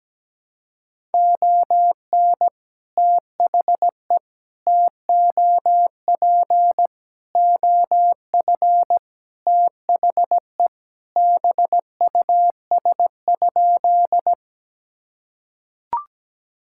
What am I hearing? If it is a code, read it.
Morse code, heard as a keyed tone, about 17 wpm, 700 Hz. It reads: ON THE TOP OF THE BUS?
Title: Sentences from Top 1000 Words - Review Part 6 - 17wpm